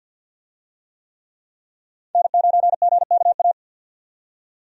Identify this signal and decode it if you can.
Morse code, heard as a keyed tone, about 53 wpm, 700 Hz. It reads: N9GKA
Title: USA Call Signs - 5 Characters - 50wpm